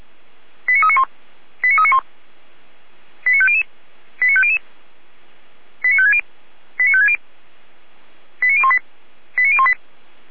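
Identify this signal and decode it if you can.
Five-tone selective call (ZVEI), heard as quick runs of five tones, tones of about 70 ms. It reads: 89391 89391 89500 89500 89560 89560 89018 89018
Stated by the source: ZVEI five-tone call received by SDR, recorded by fire brigade Feuerwehr Orlamünde.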